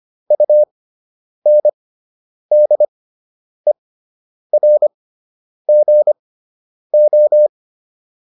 Morse code, heard as a keyed tone, about 25 wpm, spaced as for 9 wpm, 600 Hz. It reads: UNDERGO